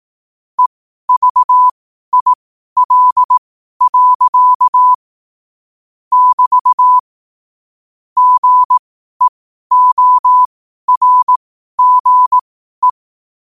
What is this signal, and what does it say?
Morse code, heard as a keyed tone, about 18 wpm, spaced as for 12 wpm, 1000 Hz. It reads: EVIL. <BT> GEORGE
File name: qotdfiles / 12wpm